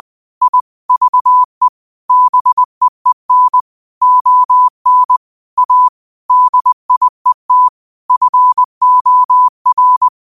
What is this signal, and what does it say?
Morse code, heard as a keyed tone, about 20 wpm, 1000 Hz. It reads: I VE BEEN ON A DIET FOR